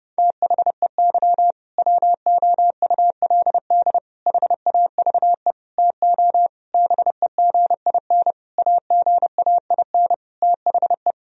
Morse code, heard as a keyed tone, about 32 wpm, 700 Hz. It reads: THEY WOULD HAVE TO BEGIN AGAIN THE
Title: Sentences from Top 1000 Words - Review Part 1 - 30wpm